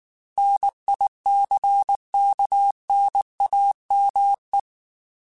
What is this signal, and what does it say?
Morse code, heard as a keyed tone, about 19 wpm, 800 Hz. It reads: NICKNAME
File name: Technical Subjects / Technical Subjects-mp3s / AND Gate @18